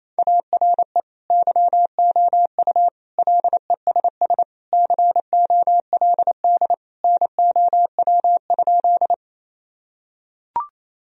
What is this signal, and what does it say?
Morse code, heard as a keyed tone, about 28 wpm, 700 Hz. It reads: ARE YOU LESS COLD NOW?
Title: Sentences from Top 1000 Words - Review Part 2 - 28wpm